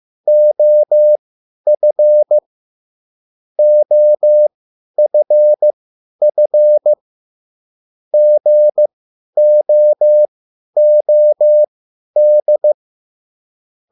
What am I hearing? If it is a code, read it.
Morse code, heard as a keyed tone, about 15 wpm, spaced as for 10 wpm, 600 Hz. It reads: OF OFF GOOD